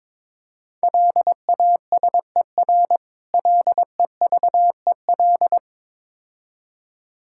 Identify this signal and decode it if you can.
Morse code, heard as a keyed tone, about 21 wpm, 690 Hz. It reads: LASER LEVEL
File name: Difficult Words with Hints - 22wpm